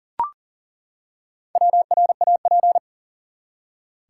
Morse code, heard as a keyed tone, about 40 wpm, 700 Hz. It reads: WRAP
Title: Four Letter Words - 40wpm